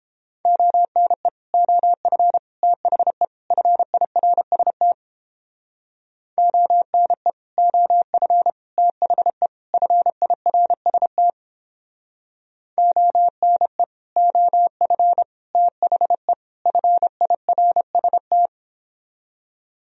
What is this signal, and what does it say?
Morse code, heard as a keyed tone, about 29 wpm, 700 Hz. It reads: ONE OF THE FIRST ONE OF THE FIRST ONE OF THE FIRST